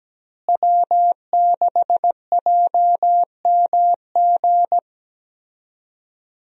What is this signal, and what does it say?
Morse code, heard as a keyed tone, about 17 wpm, 700 Hz. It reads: W6JMG
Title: USA Call Signs - 5 Characters - Repeated 2x - 17wpm